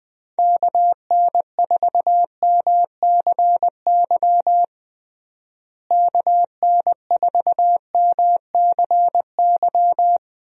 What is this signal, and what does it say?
Morse code, heard as a keyed tone, about 21 wpm, 700 Hz. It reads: KN4MCY KN4MCY
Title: USA Call Signs - 6 Characters - Repeated 2x - 20wpm